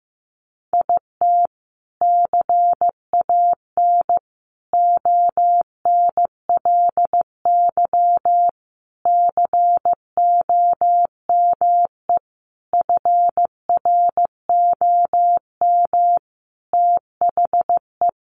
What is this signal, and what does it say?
Morse code, heard as a keyed tone, about 15 wpm, 700 Hz. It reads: IT CAN ONLY COME FROM THE
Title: Sentences from Top 1400 Words 15wpm